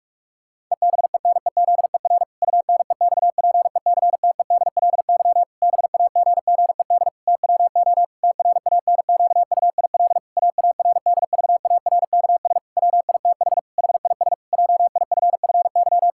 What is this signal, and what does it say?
Morse code, heard as a keyed tone, about 44 wpm, 700 Hz. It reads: EBENEZER UNEXPECTEDLY BAGGED TWO TRANQUIL AARDVARKS WITH HIS JIFFY
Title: Pangram Challenge v2 - 45wpm